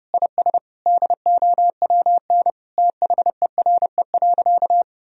Morse code, heard as a keyed tone, about 30 wpm, 700 Hz. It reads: IS DOWN THERE.
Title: Sentences from Top 1000 Words - Review Part 1 - 30wpm